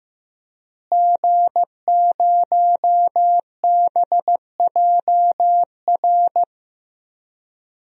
Morse code, heard as a keyed tone, about 15 wpm, 700 Hz. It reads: G0BJR
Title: DXCC-100 Call Signs - 5 Characters - Repeated 2x - 15wpm